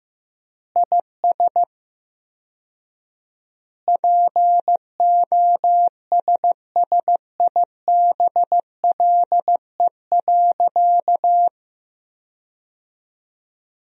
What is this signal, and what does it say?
Morse code, heard as a keyed tone, about 16 wpm, 700 Hz. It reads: IS POSSIBLE.
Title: 4x Word-Spacing - Sentences from Top 500 Words - 15wpm